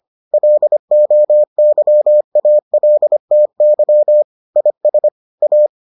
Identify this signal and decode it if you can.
Morse code, heard as a keyed tone, about 25 wpm, 600 Hz. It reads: LOYALTY IS A